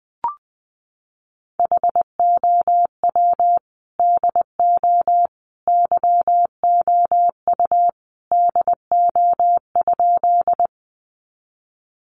Morse code, heard as a keyed tone, about 20 wpm, 700 Hz. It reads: HOW DO YOU DO?
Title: Sentences from Top 100 Words 20wpm